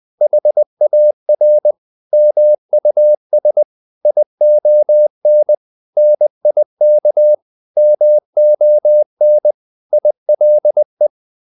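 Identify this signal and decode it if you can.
Morse code, heard as a keyed tone, about 20 wpm, 600 Hz. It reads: HAR MUS ION NIK MON ILE
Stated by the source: Morse code practice of short Danish words at 20 wpm, from CW med Gnister.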